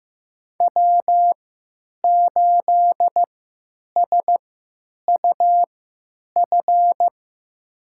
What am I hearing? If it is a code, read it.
Morse code, heard as a keyed tone, about 15 wpm, spaced as for 9 wpm, 700 Hz. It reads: W8SUF
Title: USA Call Signs - 5 Characters - 3x Character Spacing - 15wpm